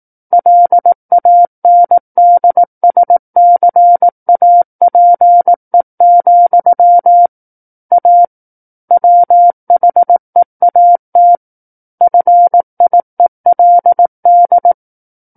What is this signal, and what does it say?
Morse code, heard as a keyed tone, about 18 wpm, 700 Hz. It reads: LANDSCAPE, A WHEAT FIELD
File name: 18 WPM